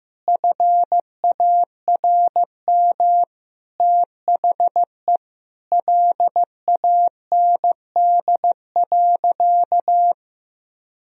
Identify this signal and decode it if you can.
Morse code, heard as a keyed tone, about 15 wpm, 700 Hz. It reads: FARM THE LAND.
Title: Sentences from Top 1000 Words - Review Part 2 - 15wpm